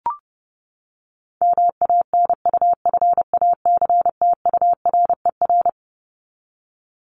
Morse code, heard as a keyed tone, about 30 wpm, 700 Hz. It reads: MANUFACTURER
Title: Top 1751-2000 Words 30wpm